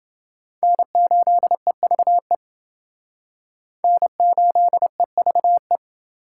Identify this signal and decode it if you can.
Morse code, heard as a keyed tone, about 29 wpm, 700 Hz. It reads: N8EVE N8EVE